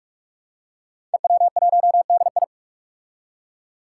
Morse code, heard as a keyed tone, about 46 wpm, 700 Hz. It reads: EW1DI
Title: Five Challenging Letter-Number Combinations 45wpm